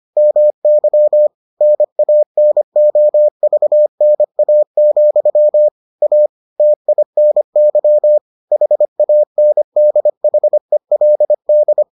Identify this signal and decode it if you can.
Morse code, heard as a keyed tone, about 25 wpm, 600 Hz. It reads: MY NANOVNA, A TINY HANDHELD